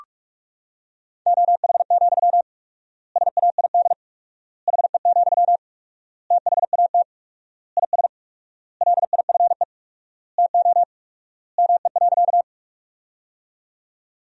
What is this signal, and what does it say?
Morse code, heard as a keyed tone, about 46 wpm, 700 Hz. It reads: OH, SAID HE, THAT IS LIFE TO ME.
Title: 4x Word-Spacing - Sentences from Top 500 Words - 45wpm